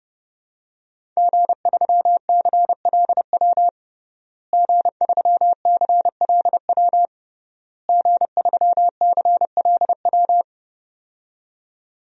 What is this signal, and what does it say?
Morse code, heard as a keyed tone, about 30 wpm, 700 Hz. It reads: G3CLW G3CLW G3CLW
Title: DXCC-100 Call Signs - 5 Characters - Repeated 3x - 30wpm